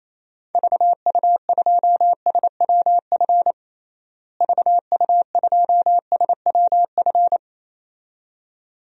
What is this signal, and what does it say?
Morse code, heard as a keyed tone, about 28 wpm, 700 Hz. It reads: VU2SWF VU2SWF
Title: DXCC-100 Call Signs - 6 Characters - Repeated 2x - 28wpm